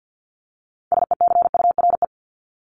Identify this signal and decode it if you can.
Morse code, heard as a keyed tone, about 50 wpm, 700 Hz. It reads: SECURE
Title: Top 1001-1250 Words 50wpm